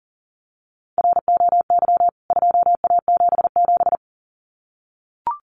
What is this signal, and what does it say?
Morse code, heard as a keyed tone, about 40 wpm, 700 Hz. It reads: ROY 2A77
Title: CWT Exchange 40wpm